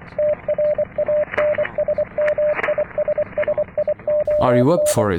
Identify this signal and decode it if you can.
Morse code, heard as a keyed tone, about 25 wpm, 600 Hz, with speech and radio noise. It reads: TRANSMISSION